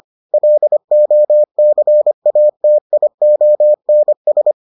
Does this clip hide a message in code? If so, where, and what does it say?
Yes: Morse code from 0.3 to 4.5 s, heard as a keyed tone, about 25 wpm, 600 Hz. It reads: LOCATIONS